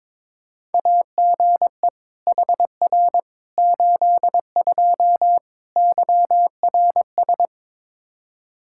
Morse code, heard as a keyed tone, about 22 wpm, 700 Hz. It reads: AGE HR 82 YRS
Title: K1USN SST QSO Practice 22wpm